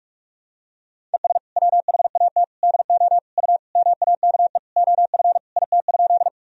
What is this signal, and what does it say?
Morse code, heard as a keyed tone, about 46 wpm, 700 Hz. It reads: ES WHAT DO U MAKE OF IT?
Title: Sentences from Top 100 Words and Abbreviations 45wpm